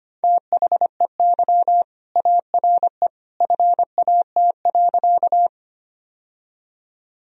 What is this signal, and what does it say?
Morse code, heard as a keyed tone, about 25 wpm, 700 Hz. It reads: THEY ARE FAT.
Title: Sentences from Top 1000 Words - Review Part 4 - 25wpm